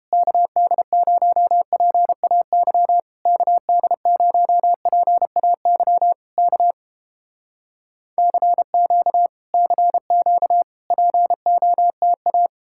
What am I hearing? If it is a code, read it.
Morse code, heard as a keyed tone, about 30 wpm, 700 Hz. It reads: KD0PAY KD0PAY K CQ CQ POTA